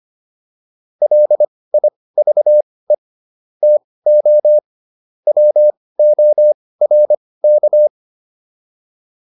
Morse code, heard as a keyed tone, about 25 wpm, spaced as for 18 wpm, 600 Hz. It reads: LIVE TO WORK